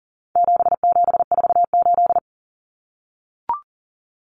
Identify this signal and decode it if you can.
Morse code, heard as a keyed tone, about 40 wpm, 700 Hz. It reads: 7748
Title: Common Cut Numbers - 4-Digit 40wpm